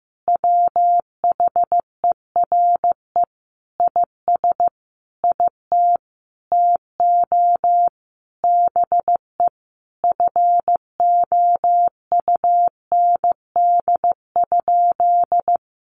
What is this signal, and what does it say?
Morse code, heard as a keyed tone, about 15 wpm, 700 Hz. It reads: WHERE IS IT TO BE FOUND?
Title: Sentences from Top 300 Words 15wpm